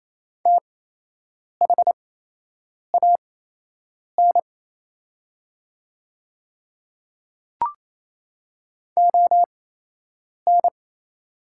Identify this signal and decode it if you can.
Morse code, heard as a keyed tone, about 28 wpm, spaced as for 8 wpm, 700 Hz. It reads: THAN ON
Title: 8x Character Spacing - Top 200 Words - 28wpm